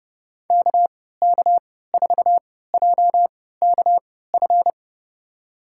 Morse code, heard as a keyed tone, about 30 wpm, spaced as for 17 wpm, 700 Hz. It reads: KK4JKF
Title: USA Call Signs - 6 Characters - 3x Character Spacing - 30wpm